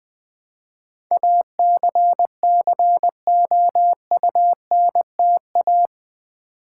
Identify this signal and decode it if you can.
Morse code, heard as a keyed tone, about 20 wpm, 700 Hz. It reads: ACCOUNTA
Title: Build A Word - 4 Letters At A Time - 11-Letter Words - 20wpm